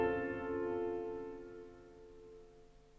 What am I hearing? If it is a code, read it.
Dial tone (North America), 350 + 440 Hz.
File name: OST 4 / 07 - Mou Hitori no Eriol